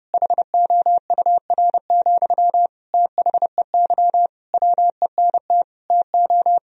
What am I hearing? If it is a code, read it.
Morse code, heard as a keyed tone, about 30 wpm, 700 Hz. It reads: HOUR, THEY WENT TO
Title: Sentences from Top 1000 Words - Review Part 2 - 30wpm